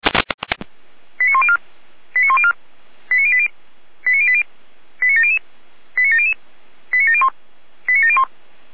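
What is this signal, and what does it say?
Five-tone selective call (ZVEI), heard as quick runs of five tones, tones of about 70 ms. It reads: 89104 89104 89080 89080 89700 89700 89701 89701
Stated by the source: ZVEI five-tone call 89020 received by SDR.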